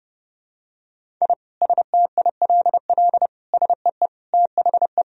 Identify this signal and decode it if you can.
Morse code, heard as a keyed tone, about 30 wpm, 700 Hz. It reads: I STILL SEE THE
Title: Sentences from Top 1000 Words - Review Part 1 - 30wpm